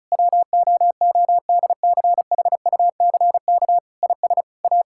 Morse code, heard as a keyed tone, about 36 wpm, 700 Hz. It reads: WOODCHUCK IS A
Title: Pangram Challenge v2 - 35wpm